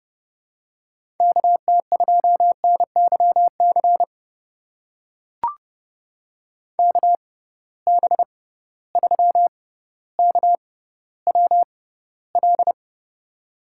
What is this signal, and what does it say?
Morse code, heard as a keyed tone, about 31 wpm, spaced as for 16 wpm, 700 Hz. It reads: KT2NYC KB3KWL
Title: USA Call Signs - 6 Characters - 6x Character Spacing - 30wpm